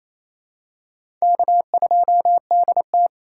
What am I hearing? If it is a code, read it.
Morse code, heard as a keyed tone, about 27 wpm, 700 Hz. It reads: K2DT
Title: USA Call Signs - 4 Characters - 4x Character Spacing - 28wpm